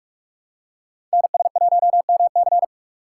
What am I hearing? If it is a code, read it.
Morse code, heard as a keyed tone, about 47 wpm, 700 Hz. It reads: NS1MC
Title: USA Call Signs - 5 Characters - 2x Character Spacing - 45wpm